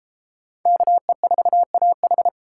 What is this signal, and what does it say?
Morse code, heard as a keyed tone, about 31 wpm, 700 Hz. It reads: KE4AH